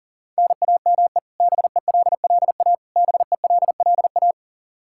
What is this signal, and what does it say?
Morse code, heard as a keyed tone, about 37 wpm, 700 Hz. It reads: NAME BELLA BELLA
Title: Practice QSO - Exchange 1 - Beginning - 40wpm